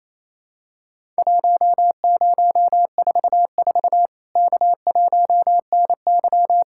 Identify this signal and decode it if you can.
Morse code, heard as a keyed tone, about 28 wpm, 700 Hz. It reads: 1044 K1NY
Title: Worked All Europe DX Contest - QTC Exchange - 28wpm